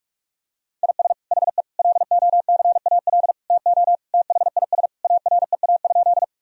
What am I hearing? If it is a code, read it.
Morse code, heard as a keyed tone, about 41 wpm, 700 Hz. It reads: IS HE LOCAL TO THIS AREA?
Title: Sentences from Top 1000 Words - Review Part 3 - 45wpm